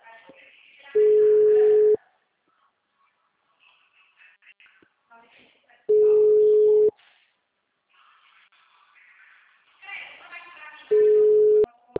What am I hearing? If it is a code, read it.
Ringback tone (Europe, 425 Hz), with speech.